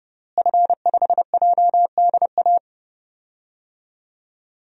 Morse code, heard as a keyed tone, about 30 wpm, 700 Hz. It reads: F5JDA
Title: DXCC-100 Call Signs - 5 Characters - Repeated 3x - 30wpm